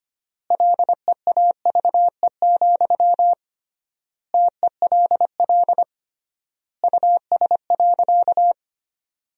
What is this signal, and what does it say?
Morse code, heard as a keyed tone, about 25 wpm, 700 Hz. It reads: LEAVE, TELL US.
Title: 3x Word-Spacing - Sentences from Top 400 Words - 25wpm